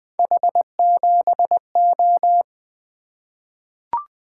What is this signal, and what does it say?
Morse code, heard as a keyed tone, about 20 wpm, 700 Hz. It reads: H7O
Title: Three Letter-Number Combinations - 20wpm